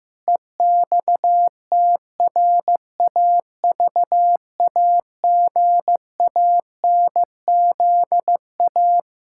Morse code, heard as a keyed tone, about 15 wpm, 700 Hz. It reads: EXTRAVAGANZA
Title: Pangram Challenge v2 - 15wpm